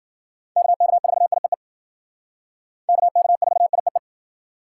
Morse code, heard as a keyed tone, about 51 wpm, 700 Hz. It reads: KK4IEE KK4IEE